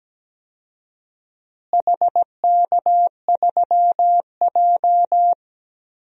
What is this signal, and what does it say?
Morse code, heard as a keyed tone, about 17 wpm, 700 Hz. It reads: HK3J
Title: DXCC-100 Call Signs - 4 Characters - Repeated 2x - 17wpm